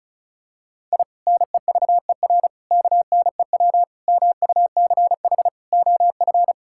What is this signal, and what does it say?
Morse code, heard as a keyed tone, about 36 wpm, 700 Hz. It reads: I NEVER KNEW MUCH OF
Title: Sentences from Top 1000 Words - Review Part 6 - 35wpm